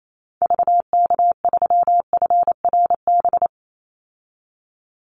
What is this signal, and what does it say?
Morse code, heard as a keyed tone, about 28 wpm, 700 Hz. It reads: VK3FRB
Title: USA Call Signs 28wpm